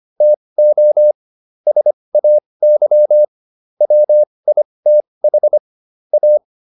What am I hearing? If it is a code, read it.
Morse code, heard as a keyed tone, about 25 wpm, spaced as for 20 wpm, 600 Hz. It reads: TO SAY WITH A